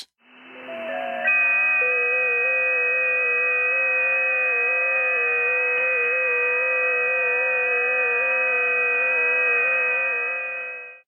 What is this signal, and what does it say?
FT8.
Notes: an audio tour of the 10 meter band!